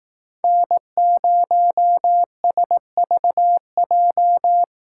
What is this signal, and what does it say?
Morse code, heard as a keyed tone, about 20 wpm, 700 Hz. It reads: N0SVJ